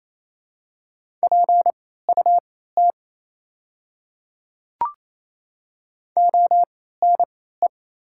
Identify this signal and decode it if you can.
Morse code, heard as a keyed tone, about 27 wpm, spaced as for 16 wpm, 690 Hz. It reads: PUT ONE
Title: 3x Character Spacing - Top 200 Words - 28wpm